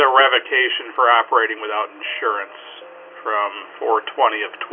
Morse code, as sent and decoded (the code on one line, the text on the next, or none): none